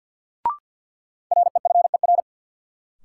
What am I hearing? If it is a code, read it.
Morse code, heard as a keyed tone, about 53 wpm, 700 Hz. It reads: REFER
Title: Sentences from Top 1000 Words - New Word Drill - 50wpm